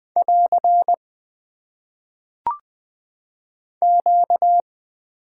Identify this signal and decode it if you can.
Morse code, heard as a keyed tone, about 20 wpm, 700 Hz. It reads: <AR> Q
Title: Top 90 QSO Elements - 6x Character Spacing - 20wpm